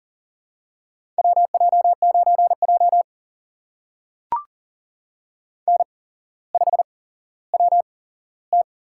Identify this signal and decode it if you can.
Morse code, heard as a keyed tone, about 39 wpm, spaced as for 17 wpm, 700 Hz. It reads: WJ9J N5WT
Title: USA Call Signs - 4 Characters - 8x Character Spacing - 40wpm